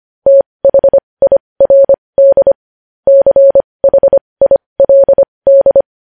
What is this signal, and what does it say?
Morse code, heard as a keyed tone, about 25 wpm, spaced as for 20 wpm, 550 Hz. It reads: THIRD CHILD
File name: podcast / News headlines-1x-25x20